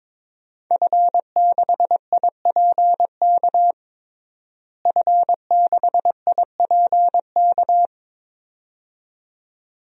Morse code, heard as a keyed tone, about 22 wpm, 700 Hz. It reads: F6IPK F6IPK
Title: DXCC-100 Call Signs - 5 Characters - Repeated 2x - 22wpm